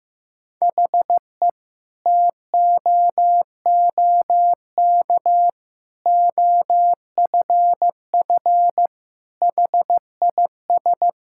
Morse code, heard as a keyed tone, about 15 wpm, 700 Hz. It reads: HE TOOK OFF HIS
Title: Sentences from Top 1600 Words 15wpm